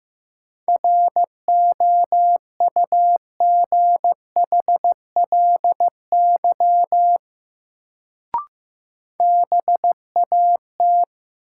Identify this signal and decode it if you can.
Morse code, heard as a keyed tone, about 15 wpm, 700 Hz. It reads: ROUGHLY BAT
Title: Sentences from Top 1500 Words - New Word Drill - 15wpm